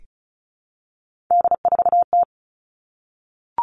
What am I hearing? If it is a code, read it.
Morse code, heard as a keyed tone, about 34 wpm, 700 Hz. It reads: D4T
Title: All Cut Numbers - 3-Digit 35wpm